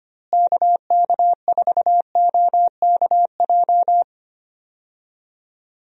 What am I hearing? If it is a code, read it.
Morse code, heard as a keyed tone, about 25 wpm, 700 Hz. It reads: KK4OKJ
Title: USA Call Signs - 6 Characters - Repeated 2x - 25wpm